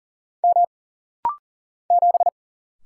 Morse code, heard as a keyed tone, about 38 wpm, 700 Hz. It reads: M 7
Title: Single Letter-Number - Rapid-Fire 40wpm